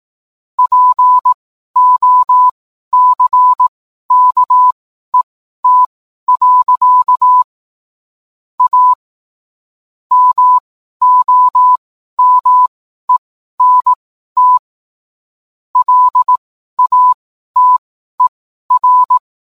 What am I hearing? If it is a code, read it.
Morse code, heard as a keyed tone, about 18 wpm, spaced as for 12 wpm, 1000 Hz. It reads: POCKET. A MOMENT LATER